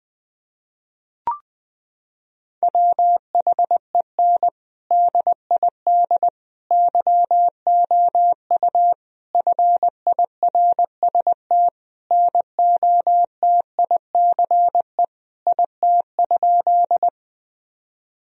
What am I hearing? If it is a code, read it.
Morse code, heard as a keyed tone, about 21 wpm, 700 Hz. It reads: WHEN DID YOU FIRST NOTICE IT?
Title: Sentences from Top 1000 Words - Review Part 2 - 20wpm